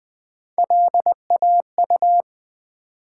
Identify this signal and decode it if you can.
Morse code, heard as a keyed tone, about 20 wpm, 700 Hz. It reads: LAU
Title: Build A Word - 3 Letters At A Time - 8-Letter Words - 20wpm